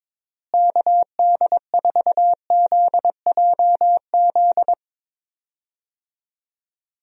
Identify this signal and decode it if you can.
Morse code, heard as a keyed tone, about 22 wpm, 700 Hz. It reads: KD4ZJZ
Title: USA Call Signs - 6 Characters - Repeated 2x - 22wpm